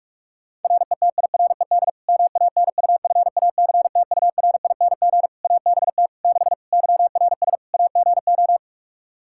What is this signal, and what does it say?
Morse code, heard as a keyed tone, about 46 wpm, 700 Hz. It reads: RETIRED MANUFACTURING ABT 6 YRS AGO
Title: Practice QSO - Exchange 3 - More Info - 45wpm